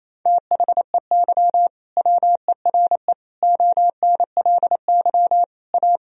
Morse code, heard as a keyed tone, about 27 wpm, 700 Hz. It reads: THEY WERE ONLY A
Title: Sentences from Top 1000 Words - Review Part 1 - 28wpm